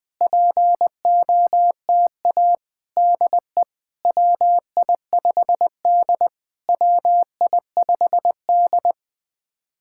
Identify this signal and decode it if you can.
Morse code, heard as a keyed tone, about 20 wpm, 700 Hz. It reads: POTA DE WI5D WI5D
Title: POTA QSO Practice - 20wpm